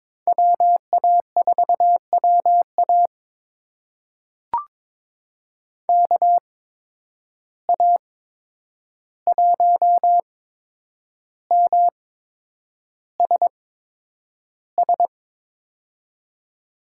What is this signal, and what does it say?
Morse code, heard as a keyed tone, about 22 wpm, spaced as for 9 wpm, 700 Hz. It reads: WA4WA KA1MSS
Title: USA Call Signs - 8x Character Spacing - 22wpm